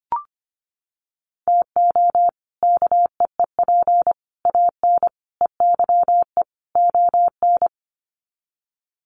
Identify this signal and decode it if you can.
Morse code, heard as a keyed tone, about 25 wpm, 700 Hz. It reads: TO KEEP AN EYE ON
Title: Sets of 5 Words Encore 25wpm